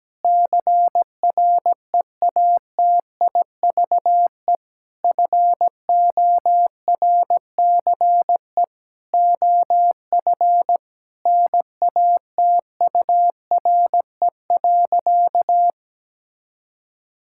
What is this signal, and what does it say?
Morse code, heard as a keyed tone, about 17 wpm, 700 Hz. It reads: CREATIVE FORCE OF NATURE.
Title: Sentences from Top 1000 Words - Review Part 5 - 17wpm